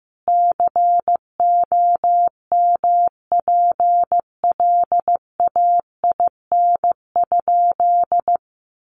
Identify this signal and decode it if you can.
Morse code, heard as a keyed tone, about 15 wpm, 700 Hz. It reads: COMPLAIN?
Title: Sentences from Top 1300 Words 15wpm